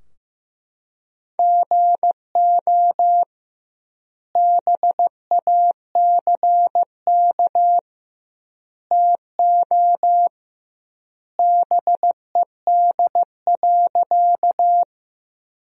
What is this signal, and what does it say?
Morse code, heard as a keyed tone, about 15 wpm, 700 Hz. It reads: GO BACK TO BED.
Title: 2x Word-Spacing - Sentences from Top 500 Words - 15wpm